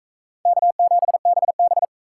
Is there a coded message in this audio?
Yes: Morse code from 0.5 to 1.8 s, heard as a keyed tone, about 37 wpm, 700 Hz. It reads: K7BB